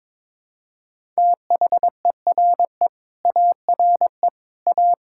Morse code, heard as a keyed tone, about 22 wpm, 700 Hz. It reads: THERE ARE A
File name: practice / 22wpm / Sentences with 4-Word Phrases - 22wpm